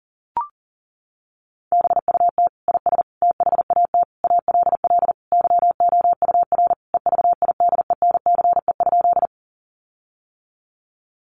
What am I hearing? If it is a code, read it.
Morse code, heard as a keyed tone, about 40 wpm, 700 Hz. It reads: BUT IS THAT ALL YOUR EVIDENCE?
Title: Sentences from Top 1300 Words 40wpm